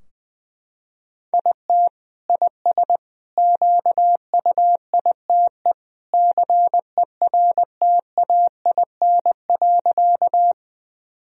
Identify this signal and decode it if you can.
Morse code, heard as a keyed tone, about 20 wpm, 700 Hz. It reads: IT IS QUITE CERTAIN.
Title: Sentences from Top 1000 Words - Review Part 2 - 20wpm